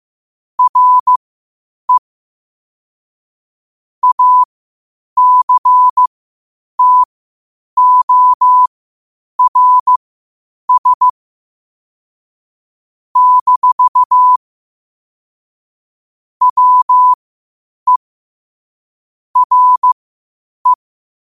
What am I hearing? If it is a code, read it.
Morse code, heard as a keyed tone, about 15 wpm, spaced as for 8 wpm, 1000 Hz. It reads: RE ACTORS - WE RE
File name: qotdfiles / 8wpm